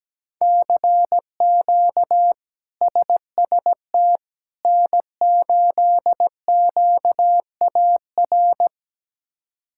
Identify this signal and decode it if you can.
Morse code, heard as a keyed tone, about 17 wpm, 700 Hz. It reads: CQ SST N8QAR
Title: K1USN SST QSO Practice - 17wpm